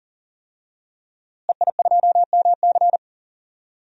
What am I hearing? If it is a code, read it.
Morse code, heard as a keyed tone, about 40 wpm, 700 Hz. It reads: EI2MC
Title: DXCC-100 Call Signs - 5 Characters - Repeated 2x - 40wpm